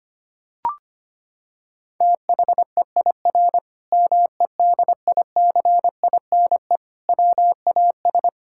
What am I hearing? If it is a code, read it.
Morse code, heard as a keyed tone, about 25 wpm, 700 Hz. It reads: THEIR MEDICINE WAS
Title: Sentences from Top 1000 Words - Review Part 6 - 25wpm